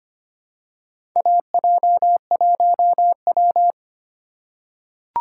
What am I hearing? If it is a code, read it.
Morse code, heard as a keyed tone, about 25 wpm, 700 Hz. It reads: AJ1W
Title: USA Call Signs - 4 Characters - 3x Character Spacing - 25wpm